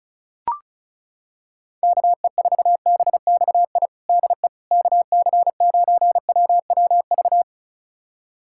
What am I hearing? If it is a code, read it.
Morse code, heard as a keyed tone, about 34 wpm, 700 Hz. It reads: KE4BXI DE KC9WWV
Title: USA Callsign DE Callsign 35wpm